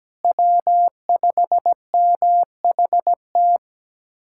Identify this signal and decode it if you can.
Morse code, heard as a keyed tone, about 17 wpm, 700 Hz. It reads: W5MHT
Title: USA Callsign DE Callsign - 5 Characters - 17wpm